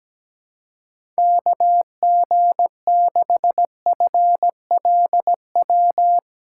Morse code, heard as a keyed tone, about 17 wpm, 700 Hz. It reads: KG6FLW
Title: USA Callsign DE Callsign 17wpm